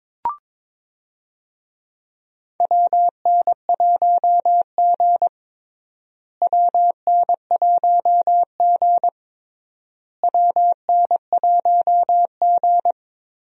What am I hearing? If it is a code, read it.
Morse code, heard as a keyed tone, about 22 wpm, 700 Hz. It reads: WN1G WN1G WN1G
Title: USA Call Signs - 4 Characters - Repeated 3x - 22wpm